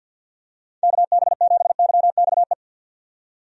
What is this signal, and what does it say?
Morse code, heard as a keyed tone, about 50 wpm, 700 Hz. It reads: KB7YXE
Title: USA Call Signs - 6 Characters - 50wpm